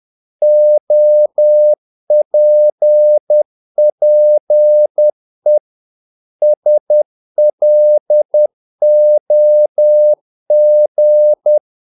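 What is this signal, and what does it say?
Morse code, heard as a keyed tone, about 10 wpm, 600 Hz. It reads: OPPE SLOG
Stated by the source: Morse practice of short Danish words at 10 wpm, from CW med Gnister.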